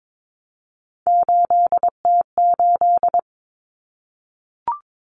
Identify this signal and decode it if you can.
Morse code, heard as a keyed tone, about 22 wpm, 700 Hz. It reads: 8T8
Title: Common Cut Numbers - 3-Digit 22wpm